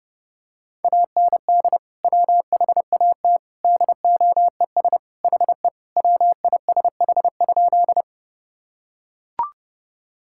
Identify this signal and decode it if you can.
Morse code, heard as a keyed tone, about 30 wpm, 700 Hz. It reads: AND WHAT DOES HE WISH?
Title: Sentences from Top 1600 Words 30wpm